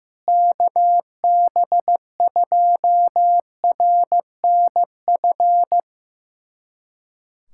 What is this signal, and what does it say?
Morse code, heard as a keyed tone, about 15 wpm, 700 Hz. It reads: KB2RNF
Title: USA Callsign DE Callsign - 6 Characters - 15wpm